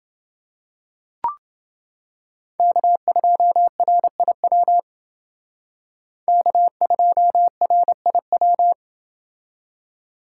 Morse code, heard as a keyed tone, about 28 wpm, 700 Hz. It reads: K2RIW K2RIW